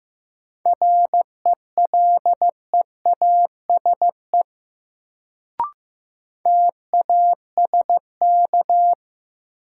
Morse code, heard as a keyed tone, about 15 wpm, 700 Hz. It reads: RELEASE TASK
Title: Sentences from Top 1100 Words - New Word Drill Alt - 15wpm